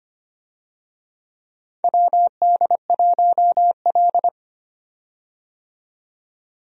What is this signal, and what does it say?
Morse code, heard as a keyed tone, about 25 wpm, 700 Hz. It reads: WD1L
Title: USA Call Signs - 4 Characters - 25wpm